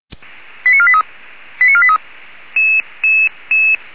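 Five-tone selective call (ZVEI), heard as quick runs of five tones, tones of about 70 ms. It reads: 89483 89483
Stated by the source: ZVEI five-tone selective call received on an SDR.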